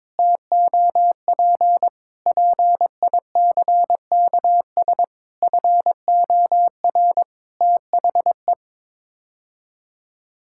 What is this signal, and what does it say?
Morse code, heard as a keyed tone, about 22 wpm, 700 Hz. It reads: TOP PICKS FOR THE
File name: Sets of 9 Words - 22wpm